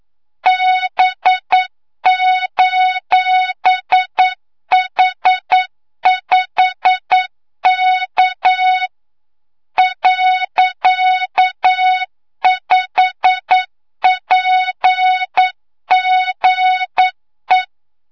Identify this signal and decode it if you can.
Morse code, heard as a keyed tone, about 9 wpm, 3700 Hz, with noise. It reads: B:H5K .5PGE
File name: Morze-HA2MN / hangp / 2 06 GYAKORLAT